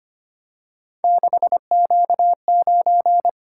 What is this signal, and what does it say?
Morse code, heard as a keyed tone, about 25 wpm, 700 Hz. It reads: 6Q9
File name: Three Letter-Number Combinations - 25wpm